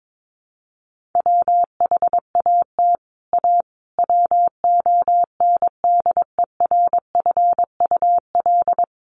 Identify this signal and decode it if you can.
Morse code, heard as a keyed tone, about 22 wpm, 700 Hz. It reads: WHAT A WONDERFUL
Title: Sentences from Top 1100 Words 22wpm